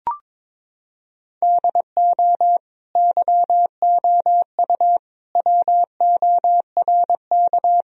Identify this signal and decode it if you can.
Morse code, heard as a keyed tone, about 22 wpm, 700 Hz. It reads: DO YOU WORK
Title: Sentences from Top 1000 Words - Review Part 2 - 22wpm